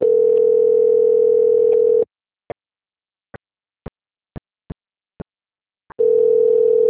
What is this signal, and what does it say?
Ringback tone (North America), 440 + 480 Hz.
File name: out- - - - - .5359